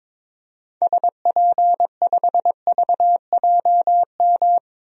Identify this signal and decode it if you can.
Morse code, heard as a keyed tone, about 22 wpm, 700 Hz. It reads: SP5VJM
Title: DXCC-100 Call Signs - 6 Characters - 22wpm